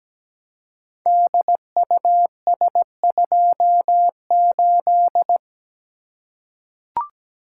Morse code, heard as a keyed tone, about 17 wpm, 700 Hz. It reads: DUS28
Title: Code Groups - 5 Letter:Numbers - 17wpm